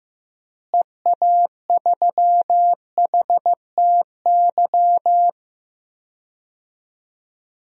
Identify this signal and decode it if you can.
Morse code, heard as a keyed tone, about 15 wpm, 700 Hz. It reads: EA3HTY
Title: DXCC-100 Call Signs - 6 Characters - Repeated 2x - 15wpm